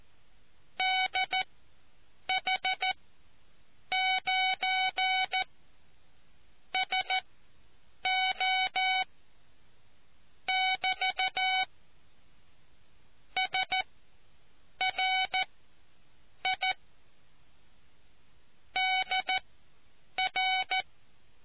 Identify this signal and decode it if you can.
Morse code, heard as a keyed tone, about 14 wpm, spaced as for 7 wpm, 750 Hz. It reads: DH9 SO <BT> SRI DR